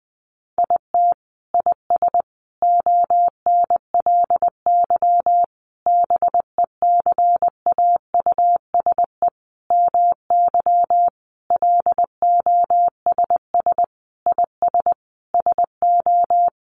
Morse code, heard as a keyed tone, about 20 wpm, 700 Hz. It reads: IT IS ONLY BECAUSE MY LOSS IS SO
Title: Sentences from Top 1400 Words 20wpm